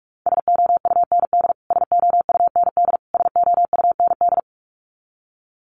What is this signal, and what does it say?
Morse code, heard as a keyed tone, about 44 wpm, 700 Hz. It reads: SOUND SOUND SOUND
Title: Top 500 Words Repeated 45wpm